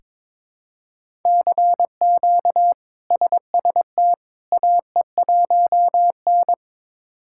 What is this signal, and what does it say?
Morse code, heard as a keyed tone, about 22 wpm, 700 Hz. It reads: CQ SST AE1N
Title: K1USN SST QSO Practice - 22wpm